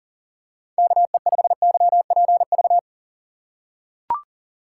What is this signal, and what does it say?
Morse code, heard as a keyed tone, about 43 wpm, 700 Hz. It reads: KE5YPV
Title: USA Call Signs - 6 Characters - 6x Character Spacing - 40wpm